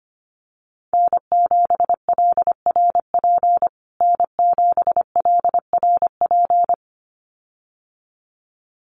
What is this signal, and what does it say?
Morse code, heard as a keyed tone, about 25 wpm, 700 Hz. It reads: N7LRP N7LRP
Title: USA Call Signs Repeated 2x 25wpm